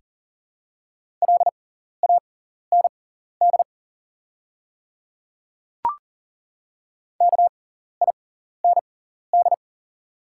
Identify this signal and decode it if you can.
Morse code, heard as a keyed tone, about 37 wpm, spaced as for 14 wpm, 700 Hz. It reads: LAND KIND
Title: 6x Character Spacing - Top 300 Words - 40wpm